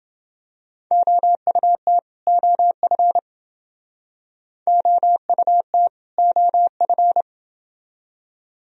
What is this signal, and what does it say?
Morse code, heard as a keyed tone, about 28 wpm, 700 Hz. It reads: OUT OF OUT OF